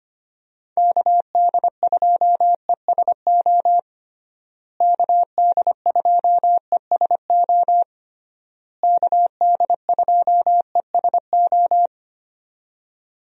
Morse code, heard as a keyed tone, about 25 wpm, 700 Hz. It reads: KD2ESO KD2ESO KD2ESO
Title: USA Call Signs - 6 Characters - Repeated 3x - 25wpm